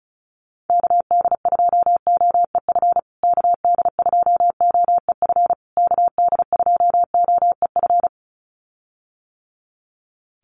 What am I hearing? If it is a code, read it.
Morse code, heard as a keyed tone, about 34 wpm, 700 Hz. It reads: KD2OEF KD2OEF KD2OEF
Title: USA Call Signs Repeated 3x 35wpm